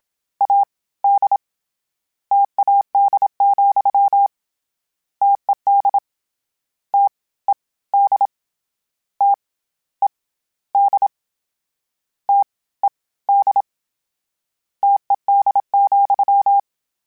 Morse code, heard as a keyed tone, about 27 wpm, 800 Hz. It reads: A D TAD, TED T E D T E D T E D TED,
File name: Variable Spacing / Variable Spacing-mp3s / T AOEs @25